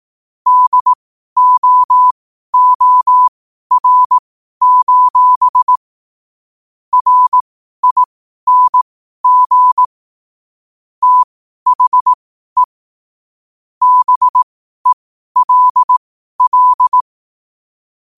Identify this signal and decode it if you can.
Morse code, heard as a keyed tone, about 18 wpm, spaced as for 12 wpm, 1000 Hz. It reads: DOOR: RING THE BELL
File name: qotdfiles / 12wpm